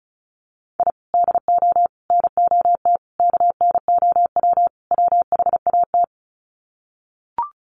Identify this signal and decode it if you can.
Morse code, heard as a keyed tone, about 34 wpm, 700 Hz. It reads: I DO NOT KNOW WHAT
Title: Sets of 5 Words 35wpm